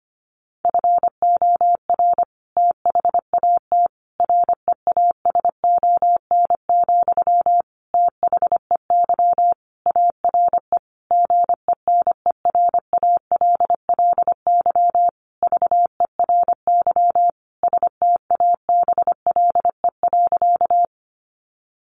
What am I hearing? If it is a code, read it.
Morse code, heard as a keyed tone, about 25 wpm, 700 Hz. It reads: FOR THAT REASON, THEY ARE GENERALLY VERY STABLE.
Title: Sentences from Top 1500 Words 25wpm